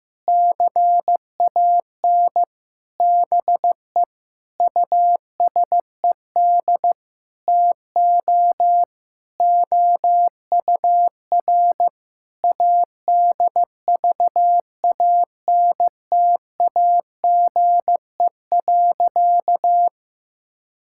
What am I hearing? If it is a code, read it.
Morse code, heard as a keyed tone, about 15 wpm, 700 Hz. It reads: CAN BE USED TO OUR ADVANTAGE.
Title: Sentences from Top 1000 Words - Review Part 6 - 15wpm